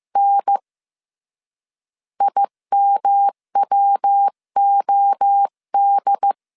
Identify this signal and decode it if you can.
Morse code, heard as a keyed tone, about 15 wpm, 800 Hz. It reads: N IMWOD
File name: CW / DET / 1 3 1500C S C 14wpm 2458